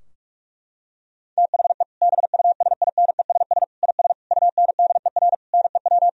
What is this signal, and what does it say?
Morse code, heard as a keyed tone, about 41 wpm, 700 Hz. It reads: THE BUSINESS IS UNDER NEW
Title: Sentences from Top 1000 Words - Review Part 4 - 45wpm